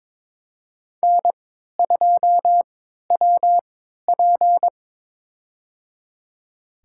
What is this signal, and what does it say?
Morse code, heard as a keyed tone, about 22 wpm, spaced as for 13 wpm, 700 Hz. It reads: N2WP
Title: USA Call Signs - 4 Characters - 3x Character Spacing - 22wpm